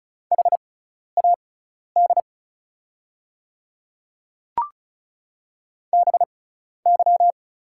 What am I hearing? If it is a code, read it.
Morse code, heard as a keyed tone, about 36 wpm, spaced as for 12 wpm, 700 Hz. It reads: HAD BY
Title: 6x Character Spacing - Top 200 Words - 35wpm